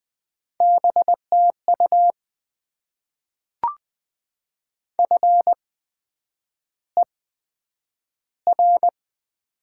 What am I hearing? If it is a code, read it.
Morse code, heard as a keyed tone, about 20 wpm, spaced as for 9 wpm, 700 Hz. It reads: BTU FER
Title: Top 20 QSO Elements - 8x Character Spacing - 20wpm